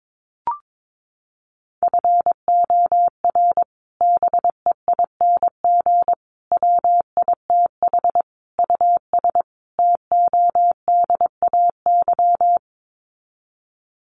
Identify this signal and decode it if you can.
Morse code, heard as a keyed tone, about 22 wpm, 700 Hz. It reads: FOR BEING WITH US TODAY
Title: Sets of 5 Words Encore2 22wpm